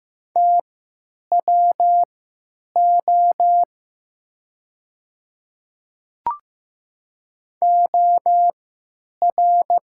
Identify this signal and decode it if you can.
Morse code, heard as a keyed tone, about 15 wpm, spaced as for 9 wpm, 690 Hz. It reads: TWO OR
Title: 3x Character Spacing - Top 100 Words - 15wpm